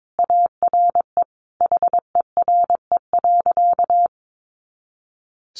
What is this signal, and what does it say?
Morse code, heard as a keyed tone, about 22 wpm, 700 Hz. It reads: ARE HERE.
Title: Sentences from Top 200 Words 22wpm